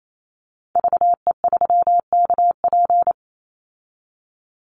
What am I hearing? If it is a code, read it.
Morse code, heard as a keyed tone, about 28 wpm, 690 Hz. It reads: VE3KP
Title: Call Signs Repeated 3x - 28wpm